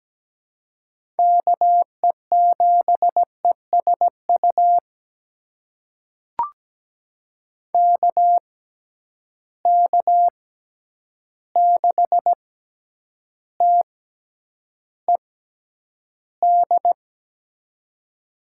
Morse code, heard as a keyed tone, about 17 wpm, spaced as for 9 wpm, 700 Hz. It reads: KE7ESU KK6TED
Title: USA Call Signs - 6 Characters - 6x Character Spacing - 17wpm